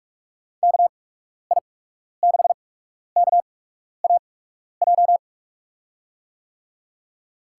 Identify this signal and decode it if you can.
Morse code, heard as a keyed tone, about 46 wpm, spaced as for 12 wpm, 700 Hz. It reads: KI6KAJ
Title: USA Call Signs - 6 Characters - 8x Character Spacing - 45wpm